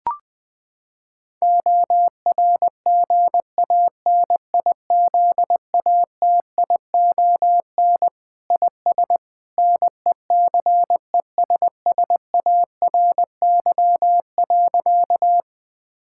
Morse code, heard as a keyed tone, about 20 wpm, 700 Hz. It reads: ORGANIZATION IS NECESSARY.
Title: Sentences from Top 1000 Words - Review Part 5 - 20wpm